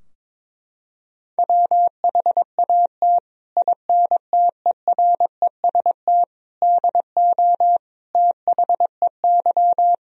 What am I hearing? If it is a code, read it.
Morse code, heard as a keyed tone, about 21 wpm, 700 Hz. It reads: WHAT INTEREST DO THEY
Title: Sentences from Top 1000 Words - Review Part 2 - 22wpm